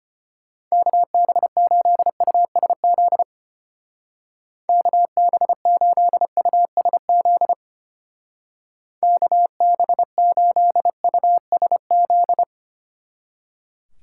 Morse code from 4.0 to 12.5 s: KB8USZ KB8USZ

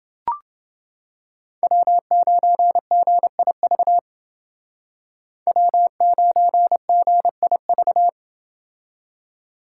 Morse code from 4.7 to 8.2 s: W9GIV

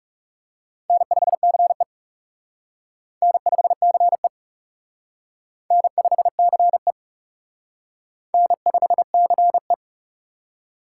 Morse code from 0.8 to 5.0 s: N5CE N5CE

